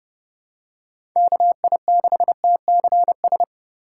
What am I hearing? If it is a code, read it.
Morse code, heard as a keyed tone, about 29 wpm, 700 Hz. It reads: KI6TCS